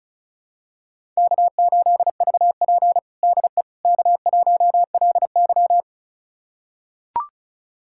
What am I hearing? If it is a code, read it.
Morse code, heard as a keyed tone, about 36 wpm, 700 Hz. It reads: K8VP DE K1LY
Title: USA Callsign DE Callsign - 4 Characters - 35wpm